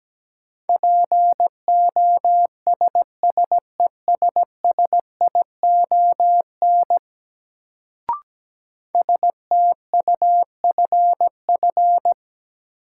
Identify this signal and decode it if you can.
Morse code, heard as a keyed tone, about 17 wpm, 700 Hz. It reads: POSSESSION STUFF
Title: Sentences from Top 1400 Words - New Word Drill - 17wpm